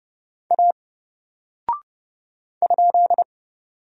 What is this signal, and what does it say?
Morse code, heard as a keyed tone, about 30 wpm, 700 Hz. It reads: A ?
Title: Single Character (Letter:Number:Punctuation:Prosign) 30wpm